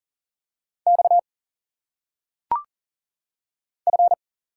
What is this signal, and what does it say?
Morse code, heard as a keyed tone, about 40 wpm, 700 Hz. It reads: X F
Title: Single Letters - Phonetic Pronunciation 40wpm